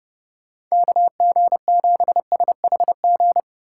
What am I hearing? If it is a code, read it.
Morse code, heard as a keyed tone, about 36 wpm, 700 Hz. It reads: KG7SHG
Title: Speed Racing - USA 2x3 Call Signs 30wpm